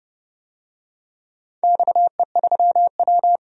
Morse code, heard as a keyed tone, about 31 wpm, 700 Hz. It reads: XE3W
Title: DXCC-100 Call Signs - 4 Characters - 30wpm